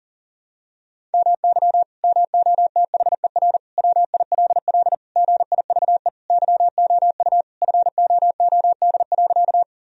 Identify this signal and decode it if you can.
Morse code, heard as a keyed tone, about 40 wpm, 700 Hz. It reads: MY MOTHER WILL GIVE YOU FOOD.
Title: Sentences from Top 1000 Words - Review Part 1 - 40wpm